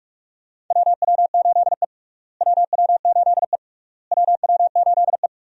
Morse code, heard as a keyed tone, about 46 wpm, 700 Hz. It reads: WW8E WW8E WW8E
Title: USA Call Signs - 4 Characters - Repeated 3x - 45wpm